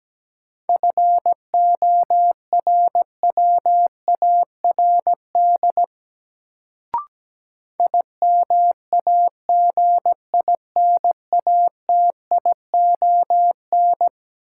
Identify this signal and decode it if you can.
Morse code, heard as a keyed tone, about 17 wpm, 700 Hz. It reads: FORWARD IMAGINATION
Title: Sentences from Top 1300 Words - New Word Drill - 17wpm